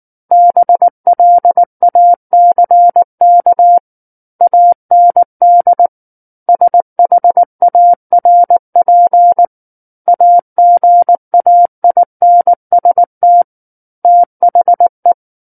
Morse code, heard as a keyed tone, about 19 wpm, 700 Hz. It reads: BLACK AND SHARP AGAINST THE
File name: Book One / 10-Chapter 10 - 19 WPM